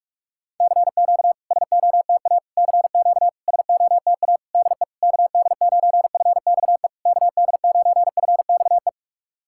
Morse code, heard as a keyed tone, about 46 wpm, 690 Hz. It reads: CQ SOTA CQ SOTA DE KD9FXE KD9FXE
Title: SOTA QSO Practice 45wpm